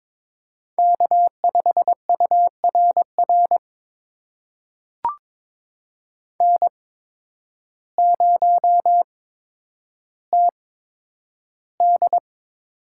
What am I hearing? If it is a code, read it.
Morse code, heard as a keyed tone, about 22 wpm, spaced as for 10 wpm, 700 Hz. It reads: K5URR N0TD
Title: USA Call Signs - 8x Character Spacing - 22wpm